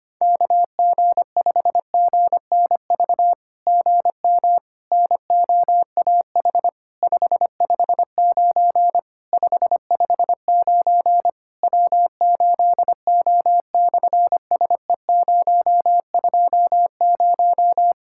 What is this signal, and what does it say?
Morse code, heard as a keyed tone, about 26 wpm, 700 Hz. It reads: KG5GNV GM NOAH 559 559 W8O/SE020
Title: SOTA QSO Practice 25wpm